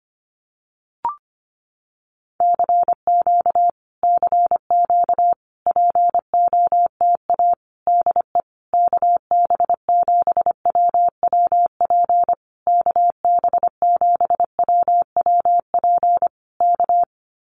Morse code, heard as a keyed tone, about 25 wpm, 700 Hz. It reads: CQ CQ POTA DE KB7WWP KB7WWP K
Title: Common QSO Phrases Encore 25wpm